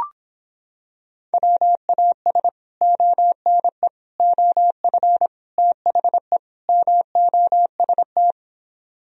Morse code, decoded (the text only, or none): WAS ONE OF THE MOST